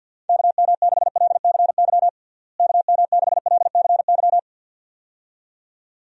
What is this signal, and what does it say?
Morse code, heard as a keyed tone, about 48 wpm, 700 Hz. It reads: KM6LCY KM6LCY
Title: USA Call Signs - 6 Characters - Repeated 2x - 50wpm